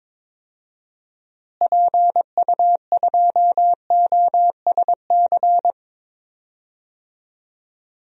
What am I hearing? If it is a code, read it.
Morse code, heard as a keyed tone, about 21 wpm, 700 Hz. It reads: PU2OSC
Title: DXCC-100 Call Signs - 6 Characters - Repeated 2x - 22wpm